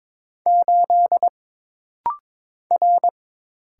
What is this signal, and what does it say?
Morse code, heard as a keyed tone, about 22 wpm, 700 Hz. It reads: 8 R
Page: This rapid-fire timing is geared to push your ICR (Instant Character Recognition) skills.